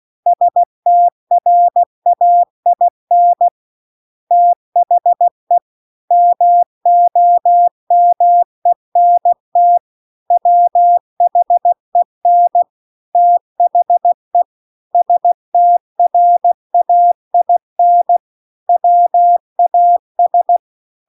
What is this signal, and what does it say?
Morse code, heard as a keyed tone, about 16 wpm, 700 Hz. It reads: STRAIN THE MOMENT WHEN THE STRAIN WAS